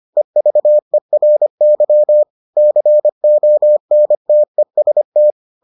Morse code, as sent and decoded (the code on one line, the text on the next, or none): . ...- . .-. -.-- / -.-. --- -. - . ... -
EVERY CONTEST